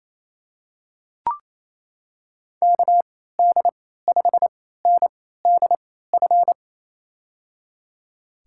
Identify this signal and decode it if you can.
Morse code, heard as a keyed tone, about 28 wpm, spaced as for 16 wpm, 700 Hz. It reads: KD5NDF
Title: USA Call Signs - 6 Characters - 3x Character Spacing - 28wpm